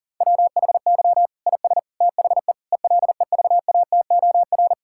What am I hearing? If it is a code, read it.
Morse code, heard as a keyed tone, about 40 wpm, 700 Hz. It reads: WHY IS THE ELEVATOR
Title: Sentences from Top 1600 Words 40wpm